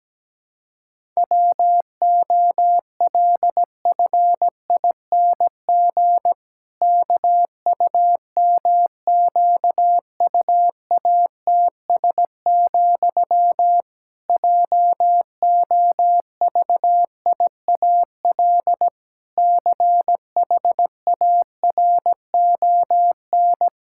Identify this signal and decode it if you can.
Morse code, heard as a keyed tone, about 17 wpm, 700 Hz. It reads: WOLFING KUMQUATS, JOVIAL CHARON
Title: Pangram Challenge v2 - 17wpm